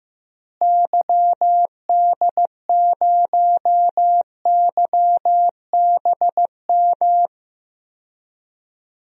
Morse code, heard as a keyed tone, about 15 wpm, 700 Hz. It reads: YD0YBM
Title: DXCC-100 Call Signs - 6 Characters - Repeated 3x - 15wpm